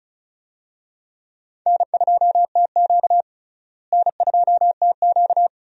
Morse code, heard as a keyed tone, about 34 wpm, 700 Hz. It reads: N2TQ N2TQ
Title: USA Call Signs - 4 Characters - Repeated 2x - 35wpm